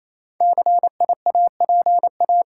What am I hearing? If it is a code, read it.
Morse code, heard as a keyed tone, about 28 wpm, 690 Hz. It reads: CIAPA
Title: Code Groups - 5 Letters - 28wpm